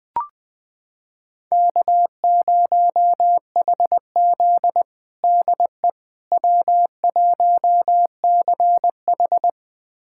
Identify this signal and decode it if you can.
Morse code, heard as a keyed tone, about 21 wpm, 700 Hz. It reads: K0HZ DE W1CH
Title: USA Callsign DE Callsign - 4 Characters - 20wpm